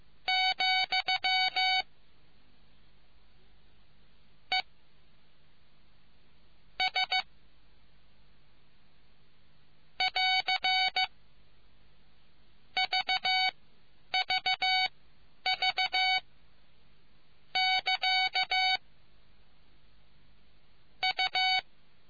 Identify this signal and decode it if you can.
Morse code, heard as a keyed tone, about 15 wpm, spaced as for 9 wpm, 750 Hz. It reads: , E S <AR> VVV <KA> U